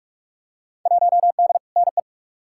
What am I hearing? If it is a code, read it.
Morse code, heard as a keyed tone, about 42 wpm, 700 Hz. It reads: 1D NE
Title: Field Day Exchanges - 45wpm